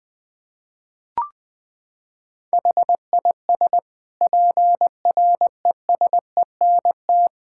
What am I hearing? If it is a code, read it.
Morse code, heard as a keyed tone, about 21 wpm, 700 Hz. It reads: HIS PRESENT
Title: Sentences from Top 1000 Words - Review Part 6 - 20wpm